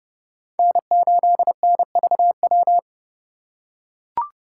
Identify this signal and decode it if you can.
Morse code, heard as a keyed tone, about 29 wpm, 700 Hz. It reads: N8NVW